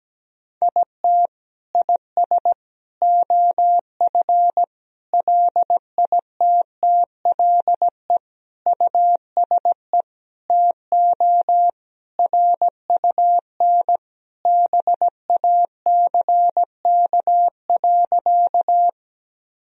Morse code, heard as a keyed tone, about 17 wpm, 700 Hz. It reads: IT IS OF LITTLE USE TO RUN BACK.
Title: Sentences from Top 1000 Words - Review Part 1 - 17wpm